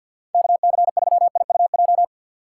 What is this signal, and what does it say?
Morse code, heard as a keyed tone, about 51 wpm, 700 Hz. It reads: KC3IUJ
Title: USA Call Signs - 2x Character Spacing - 50wpm